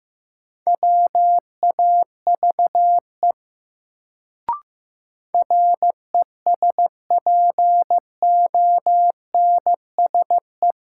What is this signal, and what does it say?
Morse code, heard as a keyed tone, about 16 wpm, 700 Hz. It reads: WAVE RESPONSE
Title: Sentences from Top 1400 Words - New Word Drill Alt - 15wpm